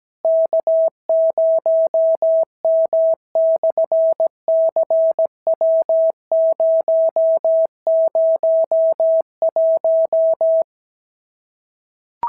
Morse code, heard as a keyed tone, about 17 wpm, 690 Hz. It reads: K0M/CW001
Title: SOTA QSO Practice 17wpm